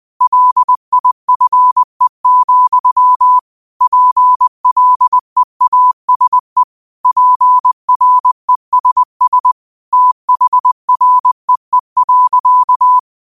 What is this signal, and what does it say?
Morse code, heard as a keyed tone, about 20 wpm, 1000 Hz. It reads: LIFE, PLEASE PRESS THREE.